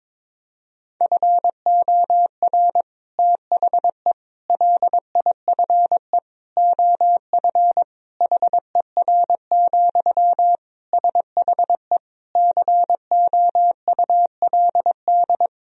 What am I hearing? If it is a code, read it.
Morse code, heard as a keyed tone, about 22 wpm, 700 Hz. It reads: FOR THE LIFE OF HER, SHE COULD
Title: Sentences from Top 1000 Words - Review Part 2 - 22wpm